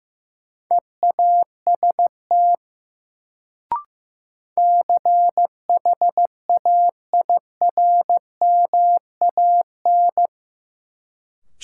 Morse code, heard as a keyed tone, about 15 wpm, 700 Hz. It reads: EAST CHAIRMAN
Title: Sets of 3 Words - New Word Drill - 15wpm